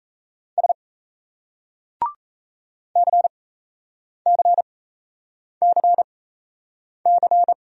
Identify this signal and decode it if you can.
Morse code, heard as a keyed tone, about 39 wpm, spaced as for 8 wpm, 700 Hz. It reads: S CCCC